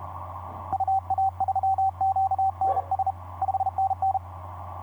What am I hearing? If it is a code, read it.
Morse code, heard as a keyed tone, about 34 wpm, 770 Hz, with radio noise. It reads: UA3QNS 5NN
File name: 8r1rpn dl8mld 15m cw etc 27oct2010 1127